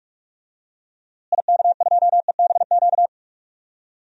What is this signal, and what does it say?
Morse code, heard as a keyed tone, about 46 wpm, 700 Hz. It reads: IK2EBQ
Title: DXCC-100 Call Signs - 6 Characters - Repeated 3x - 45wpm